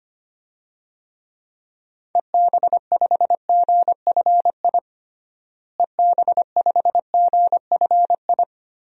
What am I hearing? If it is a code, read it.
Morse code, heard as a keyed tone, about 25 wpm, 700 Hz. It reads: EB5GFI EB5GFI
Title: DXCC-100 Call Signs - 6 Characters - Repeated 2x - 25wpm